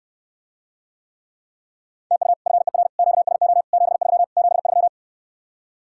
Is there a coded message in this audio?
Yes: Morse code from 2.1 to 4.9 s, heard as a keyed tone, about 69 wpm, 700 Hz. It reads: TU FER QSO 73 73